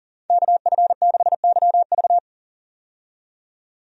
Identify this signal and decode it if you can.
Morse code, heard as a keyed tone, about 37 wpm, 700 Hz. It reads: KF6YV
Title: USA Call Signs - 5 Characters - 40wpm